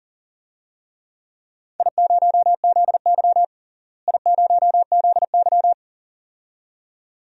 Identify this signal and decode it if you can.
Morse code, heard as a keyed tone, about 40 wpm, 700 Hz. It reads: I0ZY I0ZY
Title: DXCC-100 Call Signs - 4 Characters - Repeated 2x - 40wpm